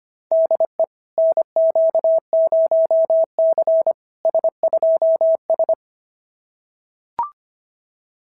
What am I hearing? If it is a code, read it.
Morse code, heard as a keyed tone, about 25 wpm, 690 Hz. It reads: DE NQ0C S2S
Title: SOTA QSO Practice 25wpm